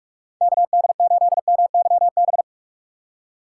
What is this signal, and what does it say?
Morse code, heard as a keyed tone, about 46 wpm, 700 Hz. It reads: KD8MYB
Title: USA Call Signs - 6x Character Spacing - 45wpm